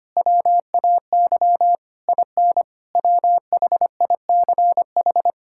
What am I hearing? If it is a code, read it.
Morse code, heard as a keyed tone, about 25 wpm, 700 Hz. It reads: WAY IN WHICH
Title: Sentences from Top 1000 Words - Review Part 5 - 25wpm